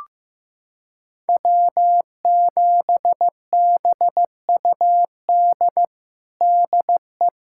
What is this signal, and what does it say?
Morse code, heard as a keyed tone, about 15 wpm, 700 Hz. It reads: W7BUD DE
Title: USA Callsign DE Callsign - 5 Characters - 15wpm